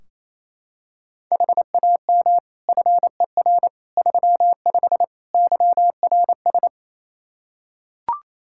Morse code, heard as a keyed tone, about 27 wpm, 700 Hz. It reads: HAM FER 35 YRS
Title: K1USN SST QSO Practice 28wpm